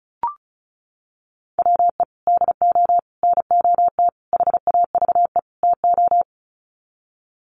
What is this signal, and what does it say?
Morse code, heard as a keyed tone, about 34 wpm, 700 Hz. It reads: WE DO NOT HAVE TO
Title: Sets of 5 Words 35wpm